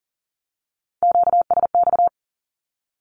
Morse code, heard as a keyed tone, about 40 wpm, 700 Hz. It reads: QSX
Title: Q-Codes Repeated 3x 40wpm